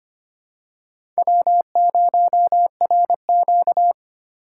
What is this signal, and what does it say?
Morse code, heard as a keyed tone, about 25 wpm, 700 Hz. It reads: W0RQ
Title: USA Call Signs - 4 Characters - 2x Character Spacing - 25wpm